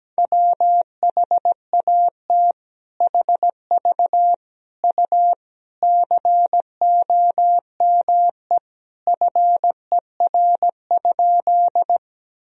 Morse code, heard as a keyed tone, about 17 wpm, 700 Hz. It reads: WHAT HV U COME FER?
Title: Sentences from Top 100 Words and Abbreviations 17wpm